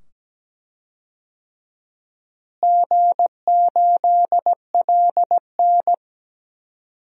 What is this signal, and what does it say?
Morse code, heard as a keyed tone, about 17 wpm, 700 Hz. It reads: G8LN
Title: DXCC-100 Call Signs - 4 Characters - Repeated 2x - 17wpm